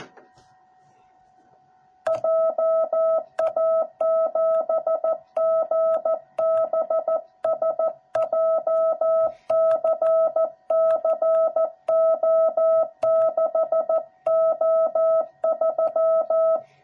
Morse code, heard as a keyed tone, about 14 wpm, 660 Hz. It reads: JA7GBS JCCO6O3
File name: jh8cbh / 2021cw / cw06b